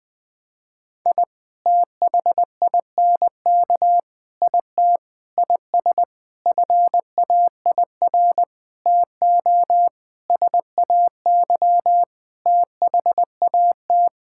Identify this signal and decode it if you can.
Morse code, heard as a keyed tone, about 20 wpm, 700 Hz. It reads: I THINK IT IS FAIR TO SAY THAT
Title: Sets of 8 Words - 20wpm